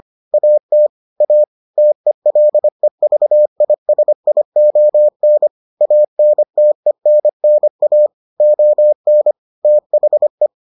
Morse code, heard as a keyed tone, about 25 wpm, 600 Hz. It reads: AT A TELEVISION ANTENNA ON THE